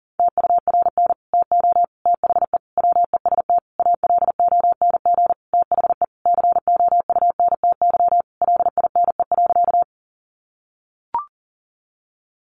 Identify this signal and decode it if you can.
Morse code, heard as a keyed tone, about 40 wpm, 700 Hz. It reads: TURN TO THE WEST ALONG THE COUNTY LINE.
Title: Sentences from Top 1200 Words 40wpm